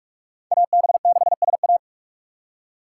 Morse code, heard as a keyed tone, about 42 wpm, 700 Hz. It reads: AB6SA
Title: USA Call Signs - 5 Characters - 4x Character Spacing - 45wpm